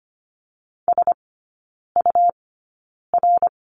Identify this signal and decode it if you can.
Morse code, heard as a keyed tone, about 25 wpm, spaced as for 9 wpm, 700 Hz. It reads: SUR
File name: course / 25:10wpm / 100 Lesson 17e - Letter u - Three Characters 25wpm